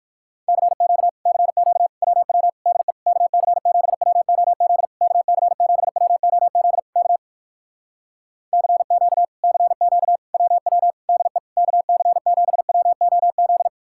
Morse code, heard as a keyed tone, about 58 wpm, 700 Hz. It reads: CQ CQ WW DE KC7WOZ KC7WOZ K CQ CQ WW DE KC7WOZ